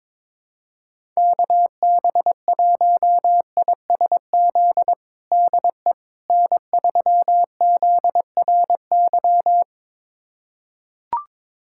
Morse code, heard as a keyed tone, about 22 wpm, 700 Hz. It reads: KB1ISZ DE N3ZRY
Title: USA Callsign DE Callsign 22wpm